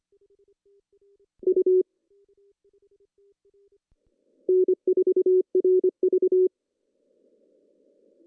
Morse code, heard as a keyed tone, about 25 wpm, 380 Hz. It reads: U N4RV
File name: CW / cqww-cw-k5tr